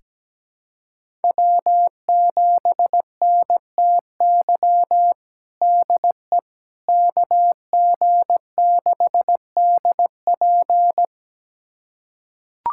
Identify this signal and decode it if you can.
Morse code, heard as a keyed tone, about 17 wpm, 700 Hz. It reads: W7NTY DE KG6DP
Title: USA Callsign DE Callsign - 5 Characters - 17wpm